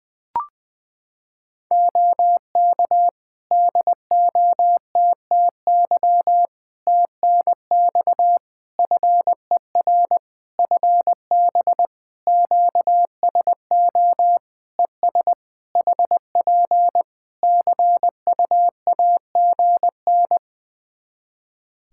Morse code, heard as a keyed tone, about 20 wpm, 700 Hz. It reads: OK DOTTY TNX FER FB QSO ES HP CUAGN
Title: K1USN SST QSO Practice 20wpm